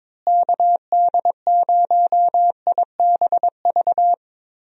Morse code, heard as a keyed tone, about 22 wpm, 700 Hz. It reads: KD0IBV
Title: USA Call Signs - 6 Characters - Repeated 2x - 22wpm